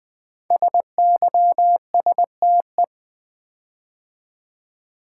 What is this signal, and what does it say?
Morse code, heard as a keyed tone, about 20 wpm, 700 Hz. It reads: SYSTE
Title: Build A Word - 5 Letters At A Time - 10-Letter Words - 20wpm